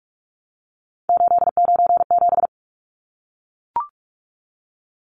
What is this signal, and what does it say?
Morse code, heard as a keyed tone, about 44 wpm, 700 Hz. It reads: 897
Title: Numbers - 3-Digit 45wpm